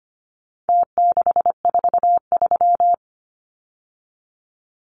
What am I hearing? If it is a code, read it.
Morse code, heard as a keyed tone, about 25 wpm, 700 Hz. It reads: T643
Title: Numbers - Contest Style - 4-Digit 25wpm